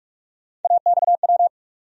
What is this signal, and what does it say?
Morse code, heard as a keyed tone, about 46 wpm, 700 Hz. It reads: AXW
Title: USA Call Sign Postfixes - 3 Letters 45wpm